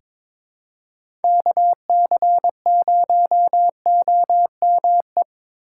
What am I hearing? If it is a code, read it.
Morse code, heard as a keyed tone, about 21 wpm, 700 Hz. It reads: KC0OME
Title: USA Call Signs - 6 Characters - 2x Character Spacing - 22wpm